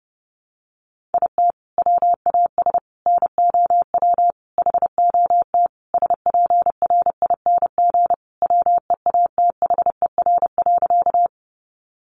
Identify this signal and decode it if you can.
Morse code, heard as a keyed tone, about 30 wpm, 700 Hz. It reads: IT WAS NOW HOT SPRING WEATHER.
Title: Sentences from Top 1200 Words 30wpm